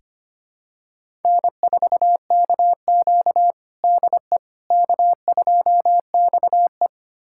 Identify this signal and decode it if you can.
Morse code, heard as a keyed tone, about 25 wpm, 700 Hz. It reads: N4KQ DE K2XE
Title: USA Callsign DE Callsign - 4 Characters - 25wpm